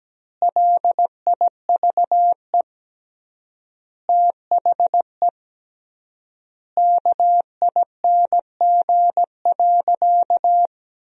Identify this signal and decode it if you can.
Morse code, heard as a keyed tone, about 17 wpm, 700 Hz. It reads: LIVE THE KING.
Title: 3x Word-Spacing - Sentences from Top 400 Words - 17wpm